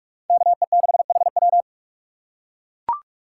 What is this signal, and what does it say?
Morse code, heard as a keyed tone, about 39 wpm, 700 Hz. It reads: KE6HW